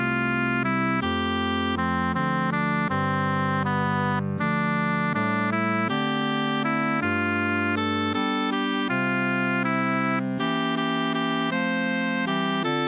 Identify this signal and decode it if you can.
There is no code.